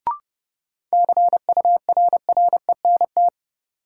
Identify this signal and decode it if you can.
Morse code, heard as a keyed tone, about 30 wpm, 700 Hz. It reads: CURRENT
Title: Sentences from Top 1600 Words - New Word Drill Alt - 30wpm